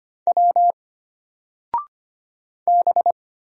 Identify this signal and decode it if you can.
Morse code, heard as a keyed tone, about 25 wpm, 700 Hz. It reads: W B
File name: Single-Letter-Number-25wpm